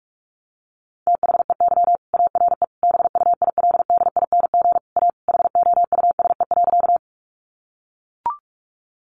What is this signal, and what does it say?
Morse code, heard as a keyed tone, about 44 wpm, 700 Hz. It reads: THEY ARE BUILDING A HOUSE.
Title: Sentences from Top 700 Words 45wpm